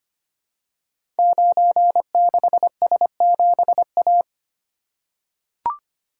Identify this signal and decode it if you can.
Morse code, heard as a keyed tone, about 25 wpm, 700 Hz. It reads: 96S7A
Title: Code Groups - 5 Letter:Numbers - 25wpm